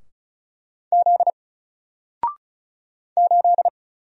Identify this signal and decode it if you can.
Morse code, heard as a keyed tone, about 33 wpm, 700 Hz. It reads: Z 8
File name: Single-Letter-Number-35wpm